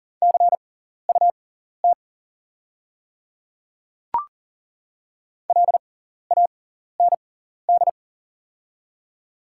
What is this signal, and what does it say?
Morse code, heard as a keyed tone, about 39 wpm, spaced as for 14 wpm, 700 Hz. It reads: CUT LAND
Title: 6x Character Spacing - Top 300 Words - 40wpm